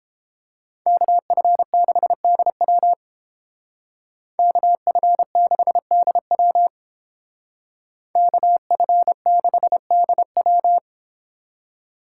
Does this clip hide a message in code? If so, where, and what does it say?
Yes: Morse code from 0.9 to 10.8 s, heard as a keyed tone, about 31 wpm, 700 Hz. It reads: KF6DW KF6DW KF6DW